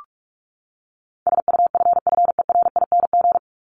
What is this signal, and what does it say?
Morse code, heard as a keyed tone, about 44 wpm, 700 Hz. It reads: SUFFERING
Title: ING Suffix - Top 100 Words with Base Words 45wpm